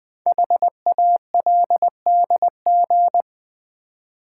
Morse code, heard as a keyed tone, about 20 wpm, 700 Hz. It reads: HALDG
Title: Code Groups - 5 Letters - 20wpm